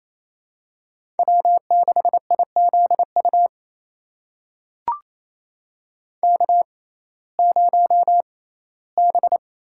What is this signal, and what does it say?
Morse code, heard as a keyed tone, about 28 wpm, spaced as for 17 wpm, 700 Hz. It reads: W6IZU K0B
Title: USA Call Signs - 6x Character Spacing - 28wpm